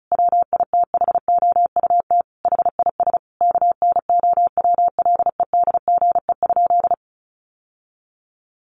Morse code, heard as a keyed tone, about 34 wpm, 700 Hz. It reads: WITHOUT HIS KNOWLEDGE?
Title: Sentences from Top 600 Words 35wpm